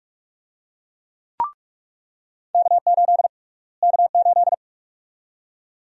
Morse code, heard as a keyed tone, about 47 wpm, 700 Hz. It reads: K8 K8
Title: USA Call Sign Prefixes - Repeated 2x - 45wpm